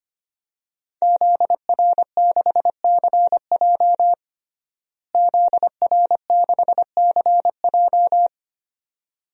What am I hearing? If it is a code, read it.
Morse code, heard as a keyed tone, about 25 wpm, 700 Hz. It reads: ZR6CJ ZR6CJ
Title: DXCC-100 Call Signs - 5 Characters - Repeated 2x - 25wpm